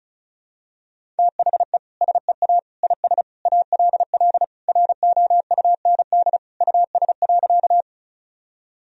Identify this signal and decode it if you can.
Morse code, heard as a keyed tone, about 34 wpm, 700 Hz. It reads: THE SEA IS ALL ROUND US.
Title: Sentences from Top 1000 Words - Review Part 1 - 35wpm